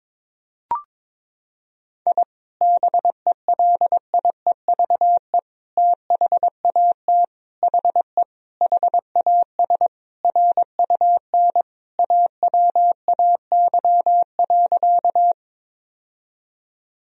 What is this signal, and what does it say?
Morse code, heard as a keyed tone, about 22 wpm, 700 Hz. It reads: I BELIEVE THAT HE HAS RUN AWAY.
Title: Sentences from Top 1000 Words - Review Part 1 - 22wpm